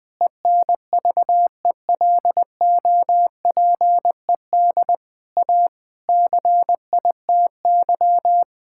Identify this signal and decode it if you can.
Morse code, heard as a keyed tone, about 20 wpm, 700 Hz. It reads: ENVELOPED A CITY
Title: Pangram Challenge v2 - 20wpm